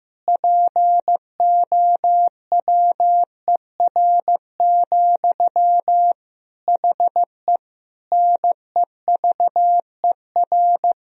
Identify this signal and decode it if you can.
Morse code, heard as a keyed tone, about 15 wpm, 700 Hz. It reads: POWER, HE NEVER
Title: Sentences from Top 1000 Words - Review Part 2 - 15wpm